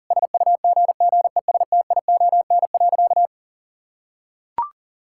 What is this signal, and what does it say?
Morse code, heard as a keyed tone, about 40 wpm, 700 Hz. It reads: SUGGESTION.
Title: Sentences from Top 1600 Words 40wpm